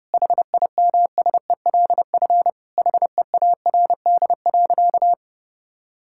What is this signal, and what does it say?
Morse code, heard as a keyed tone, about 32 wpm, 700 Hz. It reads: HIMSELF HEARD.
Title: Sentences from Top 1600 Words 30wpm